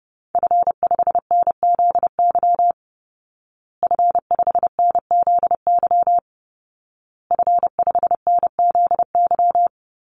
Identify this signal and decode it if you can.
Morse code, heard as a keyed tone, about 30 wpm, 700 Hz. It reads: F5NZY F5NZY F5NZY
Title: Call Signs Repeated 2x - 30wpm